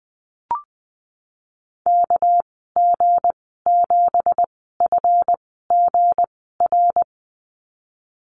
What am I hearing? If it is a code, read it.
Morse code, heard as a keyed tone, about 20 wpm, spaced as for 15 wpm, 700 Hz. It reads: KG7FGR
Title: 238 Lesson 38g - Letter q - Call Signs 20wpm